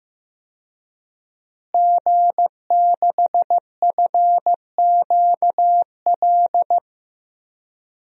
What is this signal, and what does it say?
Morse code, heard as a keyed tone, about 15 wpm, 700 Hz. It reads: G6FQL
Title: DXCC-100 Call Signs - 5 Characters - Repeated 2x - 15wpm